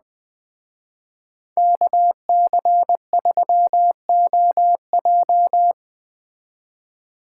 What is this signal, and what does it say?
Morse code, heard as a keyed tone, about 21 wpm, 700 Hz. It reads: KC3OJ